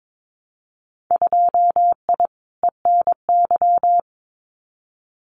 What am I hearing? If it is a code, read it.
Morse code, heard as a keyed tone, about 22 wpm, 700 Hz. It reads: 2I ENY
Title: Winter Field Day Exchanges Encore 22wpm